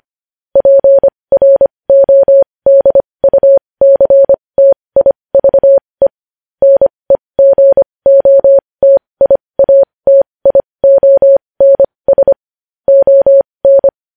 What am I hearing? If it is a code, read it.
Morse code, heard as a keyed tone, about 25 wpm, spaced as for 20 wpm, 550 Hz. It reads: PRODUCTIVE NEGOTIATIONS ON
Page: news headlines in morse code at 20 wpm